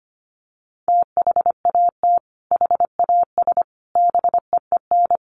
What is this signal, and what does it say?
Morse code, heard as a keyed tone, about 25 wpm, 700 Hz. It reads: THAT HAS BEEN
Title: Sentences from Top 700 Words 25wpm